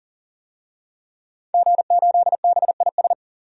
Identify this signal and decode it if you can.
Morse code, heard as a keyed tone, about 40 wpm, 700 Hz. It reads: G8BIS
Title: DXCC-100 Call Signs - 5 Characters - 40wpm